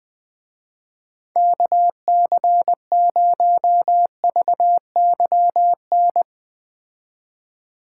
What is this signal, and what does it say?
Morse code, heard as a keyed tone, about 19 wpm, 700 Hz. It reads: KC0VYN
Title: Speed Racing - USA 2x3 Call Signs 17wpm